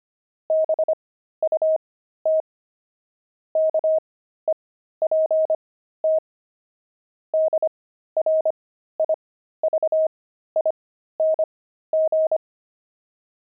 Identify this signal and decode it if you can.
Morse code, heard as a keyed tone, about 25 wpm, spaced as for 13 wpm, 630 Hz. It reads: BUT KEPT DRIVING
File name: pangram-fs-13-cs-25-jaded